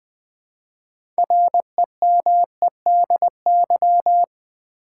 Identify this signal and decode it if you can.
Morse code, heard as a keyed tone, about 20 wpm, 700 Hz. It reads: REMEDY
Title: 6 Letter Words - First 3 Letters Given - 20wpm